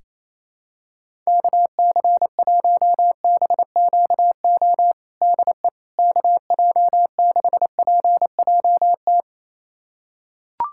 Morse code, heard as a keyed tone, about 28 wpm, 700 Hz. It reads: KC1BQO DE KJ6PJT
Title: USA Callsign DE Callsign - 6 Characters - 28wpm